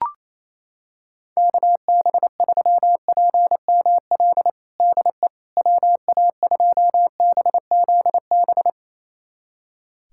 Morse code, heard as a keyed tone, about 28 wpm, 700 Hz. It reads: KB3PML DE WA2BZB
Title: USA Callsign DE Callsign - 6 Characters - 28wpm